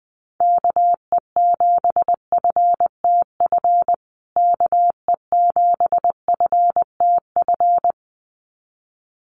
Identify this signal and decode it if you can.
Morse code, heard as a keyed tone, about 20 wpm, 700 Hz. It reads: KE7FTF KE7FTF
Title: USA Call Signs Repeated 2x 20wpm